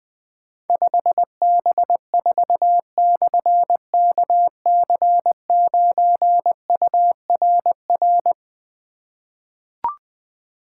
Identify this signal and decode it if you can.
Morse code, heard as a keyed tone, about 20 wpm, 700 Hz. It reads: 5B4/KC9URR
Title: Difficult Call Signs 20wpm